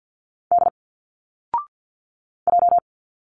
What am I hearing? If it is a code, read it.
Morse code, heard as a keyed tone, about 47 wpm, 700 Hz. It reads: D J
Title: Single Letters 50wpm